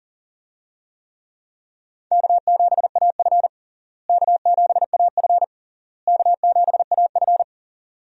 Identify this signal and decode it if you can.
Morse code, heard as a keyed tone, about 37 wpm, 700 Hz. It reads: K7AF K7AF K7AF
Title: USA Call Signs - 4 Characters - Repeated 3x - 40wpm